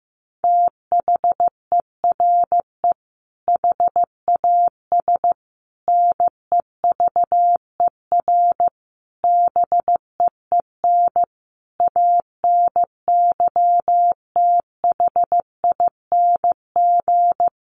Morse code, heard as a keyed tone, about 15 wpm, 700 Hz. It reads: THERE HAS NEVER BEEN ANYTHING
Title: Sentences from Top 1500 Words 15wpm